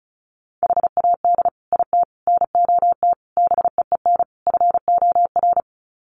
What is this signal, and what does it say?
Morse code, heard as a keyed tone, about 34 wpm, 700 Hz. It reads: HAD IT NOT BEEN FOR
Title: Sets of 5 Words Encore2 35wpm